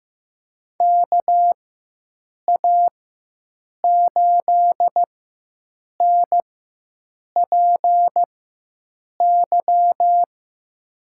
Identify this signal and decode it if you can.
Morse code, heard as a keyed tone, about 15 wpm, spaced as for 7 wpm, 700 Hz. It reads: KA8NPY
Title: USA Call Signs - 4x Character Spacing - 15wpm